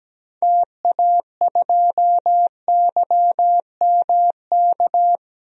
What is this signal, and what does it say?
Morse code, heard as a keyed tone, about 17 wpm, 700 Hz. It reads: TA2YMK
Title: DXCC-100 Call Signs - 6 Characters - 17wpm